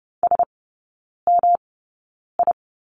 Morse code, heard as a keyed tone, about 30 wpm, spaced as for 9 wpm, 700 Hz. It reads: SMI
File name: Lesson 19e - Letter w - Three Characters 30wpm